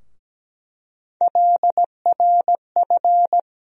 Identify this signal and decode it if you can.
Morse code, heard as a keyed tone, about 17 wpm, 700 Hz. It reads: LRF
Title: USA Call Sign Postfixes - 3 Letters 17wpm